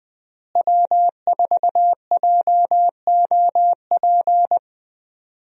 Morse code, heard as a keyed tone, about 21 wpm, 700 Hz. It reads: W4JOP
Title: USA Call Signs - 5 Characters - Repeated 3x - 20wpm